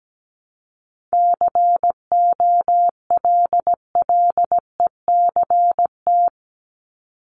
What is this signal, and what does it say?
Morse code, heard as a keyed tone, about 17 wpm, 700 Hz. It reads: COLLECT
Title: Top 1501-1750 Words 17wpm